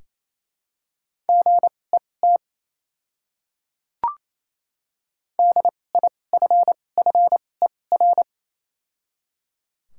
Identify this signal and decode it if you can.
Morse code, heard as a keyed tone, about 27 wpm, spaced as for 20 wpm, 700 Hz. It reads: GET DIFFER
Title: 2x Character Spacing - Top 200 Words - 28wpm